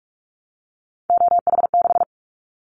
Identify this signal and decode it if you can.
Morse code, heard as a keyed tone, about 46 wpm, 700 Hz. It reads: OH6
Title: Callsign Prefixes for Top 50 DXCC - 45wpm